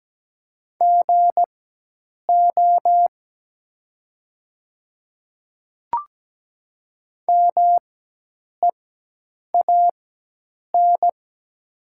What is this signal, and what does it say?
Morse code, heard as a keyed tone, about 17 wpm, spaced as for 8 wpm, 700 Hz. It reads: GO MEAN
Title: 4x Character Spacing - Top 300 Words - 17wpm